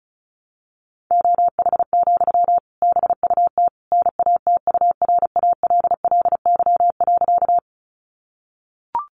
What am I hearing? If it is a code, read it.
Morse code, heard as a keyed tone, about 34 wpm, 700 Hz. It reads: OH, BUT NATURALLY.
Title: Sentences from Top 1300 Words 35wpm